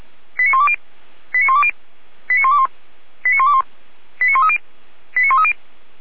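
Five-tone selective call (ZVEI), heard as quick runs of five tones, tones of about 70 ms. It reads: 89120 89120 89121 89121 89130 89130
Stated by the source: ZVEI five-tone selective call received on SDR.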